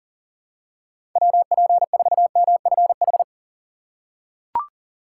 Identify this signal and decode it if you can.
Morse code, heard as a keyed tone, about 37 wpm, 700 Hz. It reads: WP4MFH
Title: USA Call Signs - 6 Characters - 2x Character Spacing - 40wpm